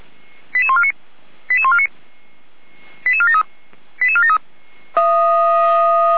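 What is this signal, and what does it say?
Five-tone selective call (ZVEI), heard as quick runs of five tones, tones of about 70 ms. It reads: 88149 88149 88473 88473
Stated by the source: SDR-received ZVEI five-tone selective call, code 88150.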